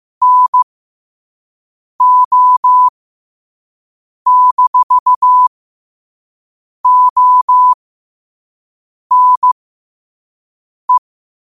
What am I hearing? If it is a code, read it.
Morse code, heard as a keyed tone, about 15 wpm, spaced as for 5 wpm, 1000 Hz. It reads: NO-ONE
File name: qotdfiles / 5wpm